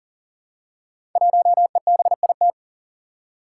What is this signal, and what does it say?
Morse code, heard as a keyed tone, about 39 wpm, 700 Hz. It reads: 1EBIT
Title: Five Challenging Letter-Number Combinations 40wpm